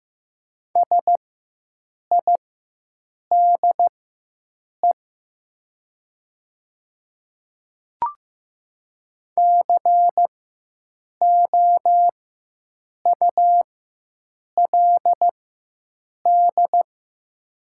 Morse code, heard as a keyed tone, about 15 wpm, spaced as for 7 wpm, 700 Hz. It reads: SIDE COULD
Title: 4x Character Spacing - Top 200 Words - 15wpm